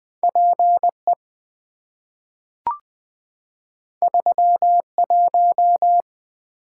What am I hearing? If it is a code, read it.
Morse code, heard as a keyed tone, about 20 wpm, 700 Hz. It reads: PE 31
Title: Two Letter-Number Combinations - 20wpm